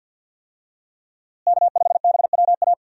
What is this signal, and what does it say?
Morse code, heard as a keyed tone, about 51 wpm, 700 Hz. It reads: K5BWA
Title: USA Call Signs - 5 Characters - Repeated 3x - 50wpm